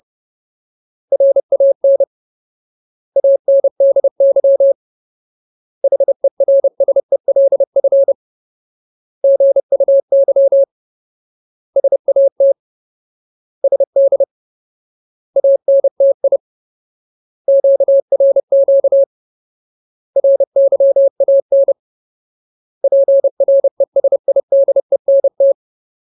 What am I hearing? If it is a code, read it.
Morse code, heard as a keyed tone, about 30 wpm, 550 Hz. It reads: RAN ANDY HERSELF GUY SAT SD ANTI QRQ RYAN PRESIDENT
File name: Ultimate Word List / 30WPM Ultimate